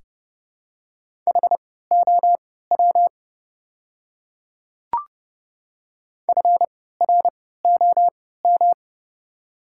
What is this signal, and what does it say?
Morse code, heard as a keyed tone, about 30 wpm, spaced as for 17 wpm, 700 Hz. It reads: HOW FROM
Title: 3x Character Spacing - Top 100 Words - 30wpm